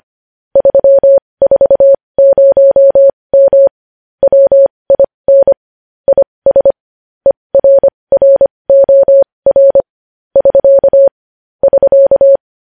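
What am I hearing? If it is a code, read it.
Morse code, heard as a keyed tone, about 25 wpm, spaced as for 20 wpm, 550 Hz. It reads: 340M WIN IS ERROR <SK> <SK>
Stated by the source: Morse practice of news headlines at 20 wpm.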